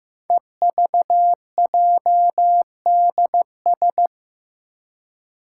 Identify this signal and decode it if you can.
Morse code, heard as a keyed tone, about 15 wpm, 700 Hz. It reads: EVJDS
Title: Code Groups - 5 Letters - 15wpm